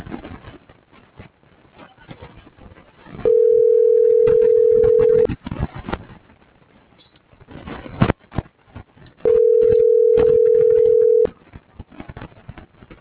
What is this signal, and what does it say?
Ringback tone (North America), 440 + 480 Hz.